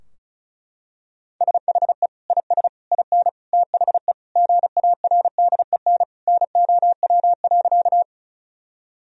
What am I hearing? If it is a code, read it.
Morse code, heard as a keyed tone, about 36 wpm, 700 Hz. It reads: SHE IS IN THE GARDEN NOW.
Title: Sentences from Top 1000 Words - Review Part 4 - 35wpm